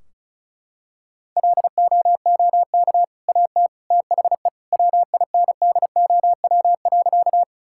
Morse code, heard as a keyed tone, about 36 wpm, 700 Hz. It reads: LOOK AT THE WINDOW.
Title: Sentences from Top 1600 Words 35wpm